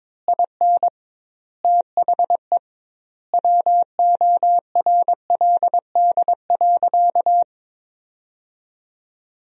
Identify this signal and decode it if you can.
Morse code, heard as a keyed tone, about 23 wpm, 700 Hz. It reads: IN THE WORLD.
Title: 2x Word-Spacing - Sentences from Top 200 Words - 22wpm